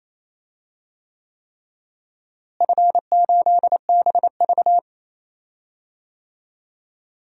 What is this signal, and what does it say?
Morse code, heard as a keyed tone, about 28 wpm, 700 Hz. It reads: F8BV
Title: DXCC-100 Call Signs - 4 Characters - Repeated 3x - 28wpm